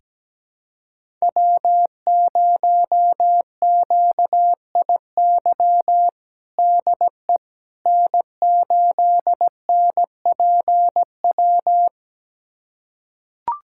Morse code, heard as a keyed tone, about 17 wpm, 700 Hz. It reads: W0QIY DE N8NPW
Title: USA Callsign DE Callsign - 5 Characters - 17wpm